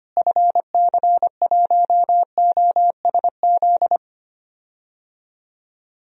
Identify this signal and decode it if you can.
Morse code, heard as a keyed tone, about 25 wpm, 700 Hz. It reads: FC1OSZ
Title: DXCC-100 Call Signs - 6 Characters - Repeated 3x - 25wpm